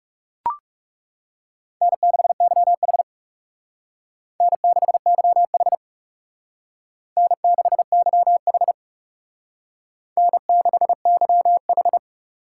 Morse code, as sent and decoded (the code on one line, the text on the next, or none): -. -.... -.-- .... / -. -.... -.-- .... / -. -.... -.-- .... / -. -.... -.-- ....
N6YH N6YH N6YH N6YH